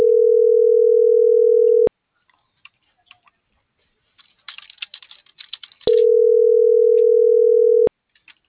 Ringback tone (North America), 440 + 480 Hz.